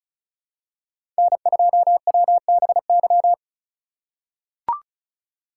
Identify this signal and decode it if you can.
Morse code, heard as a keyed tone, about 34 wpm, 700 Hz. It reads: N2WBY
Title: USA Call Signs - 5 Characters - 2x Character Spacing - 35wpm